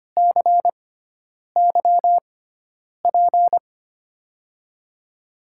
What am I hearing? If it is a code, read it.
Morse code, heard as a keyed tone, about 25 wpm, spaced as for 9 wpm, 700 Hz. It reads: CYP
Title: USA Call Signs - 6 Characters - 6x Character Spacing - 25wpm